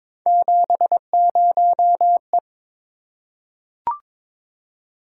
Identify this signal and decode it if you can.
Morse code, heard as a keyed tone, about 22 wpm, 700 Hz. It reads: 70E
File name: practice / 22wpm / Three Letter-Number Combinations - 22wpm